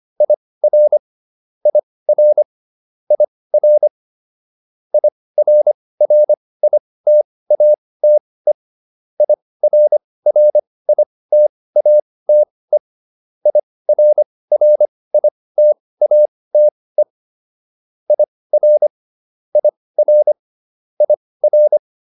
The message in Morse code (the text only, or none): IR IR IR IRRITATE IRRITATE IRRITATE IR IR IR